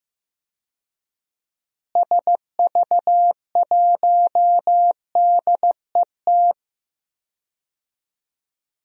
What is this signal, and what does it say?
Morse code, heard as a keyed tone, about 15 wpm, 700 Hz. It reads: SV1DET
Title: DXCC-100 Call Signs - 6 Characters - Repeated 3x - 15wpm